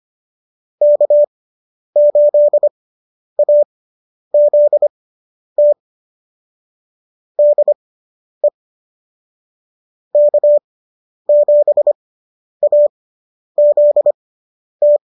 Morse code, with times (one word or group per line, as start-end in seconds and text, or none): 0.8-5.7 K8AZT
7.4-8.5 DE
10.2-15.0 K7AZT